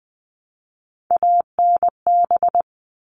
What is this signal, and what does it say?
Morse code, heard as a keyed tone, about 20 wpm, 700 Hz. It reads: ANB
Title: All Cut Numbers - 3-Digit 20wpm